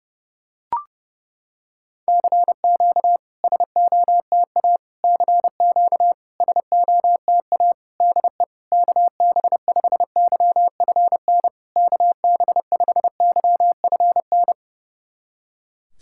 Morse code, heard as a keyed tone, about 30 wpm, 700 Hz. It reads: CQ SOTA CQ SOTA DE KB5YFN KB5YFN
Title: SOTA QSO Practice 30wpm